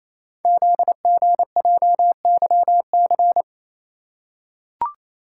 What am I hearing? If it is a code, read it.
Morse code, heard as a keyed tone, about 27 wpm, 700 Hz. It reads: ZGJYC
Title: Code Groups - 5 Letter:Numbers - 28wpm